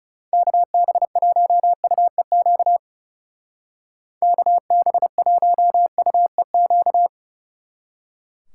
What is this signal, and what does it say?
Morse code, heard as a keyed tone, about 34 wpm, 700 Hz. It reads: KB1UEQ KB1UEQ
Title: Speed Racing - USA 2x3 Call Signs 30wpm